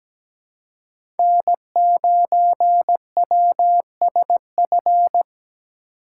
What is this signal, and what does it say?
Morse code, heard as a keyed tone, about 17 wpm, 700 Hz. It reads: N 9 W S F
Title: USA Call Signs - 5 Characters - 6x Character Spacing - 17wpm